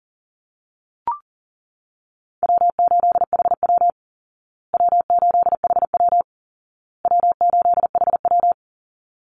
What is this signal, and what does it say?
Morse code, heard as a keyed tone, about 40 wpm, 700 Hz. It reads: W8HW W8HW W8HW
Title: Easy Call Signs Repeated 3x - 40wpm